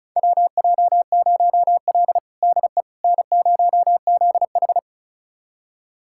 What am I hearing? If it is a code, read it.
Morse code, heard as a keyed tone, about 34 wpm, 700 Hz. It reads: WJ0L DE N0ZH
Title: USA Callsign DE Callsign - 4 Characters - 35wpm